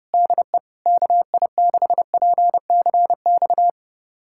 Morse code, heard as a keyed tone, about 32 wpm, 700 Hz. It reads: DE KI6PCX
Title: USA Callsign DE Callsign - 6 Characters - 30wpm